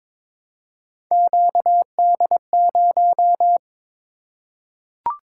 Morse code, heard as a keyed tone, about 22 wpm, 700 Hz. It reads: QD0
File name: practice / 22wpm / Three Letter-Number Combinations - 22wpm